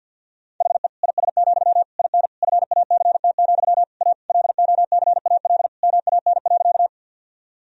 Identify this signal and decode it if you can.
Morse code, heard as a keyed tone, about 50 wpm, 700 Hz. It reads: HE IS, IN FACT, A LOCAL MAN.
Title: Sentences from Top 1000 Words - Review Part 3 - 50wpm